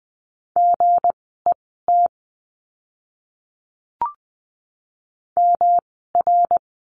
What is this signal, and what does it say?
Morse code, heard as a keyed tone, about 20 wpm, spaced as for 14 wpm, 700 Hz. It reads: GET MR
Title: 244 Lesson 39f - Letter z - Words 20wpm